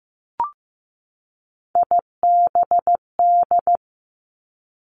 Morse code, heard as a keyed tone, about 15 wpm, 700 Hz. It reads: IBD
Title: Three Challenging Letter-Number Combinations 15wpm